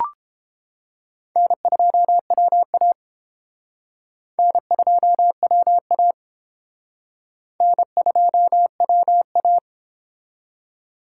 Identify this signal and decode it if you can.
Morse code, heard as a keyed tone, about 30 wpm, 700 Hz. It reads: N2WA N2WA N2WA